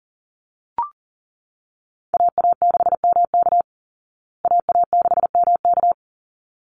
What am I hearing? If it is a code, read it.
Morse code, heard as a keyed tone, about 40 wpm, 700 Hz. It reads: AA6MK AA6MK
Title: Call Signs Repeated 3x - 40wpm